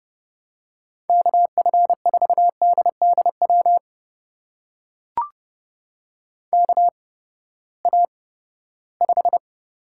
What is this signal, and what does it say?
Morse code, heard as a keyed tone, about 30 wpm, spaced as for 17 wpm, 700 Hz. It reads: KF4DDW KA5
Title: USA Call Signs - 6 Characters - 8x Character Spacing - 30wpm